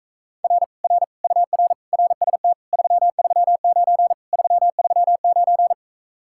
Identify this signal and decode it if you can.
Morse code, heard as a keyed tone, about 45 wpm, 700 Hz. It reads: R R UR RST 339 339